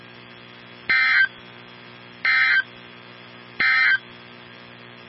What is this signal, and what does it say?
Emergency Alert System (SAME) tones, heard as harsh buzzing bursts. It is heard from 0.8 to 4.1 s.
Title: NOAA Weather Radio 162.550